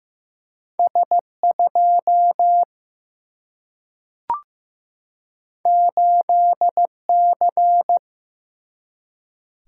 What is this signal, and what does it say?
Morse code, heard as a keyed tone, about 15 wpm, 700 Hz. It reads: S2 8C
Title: Single-Letter Single-Number 15wpm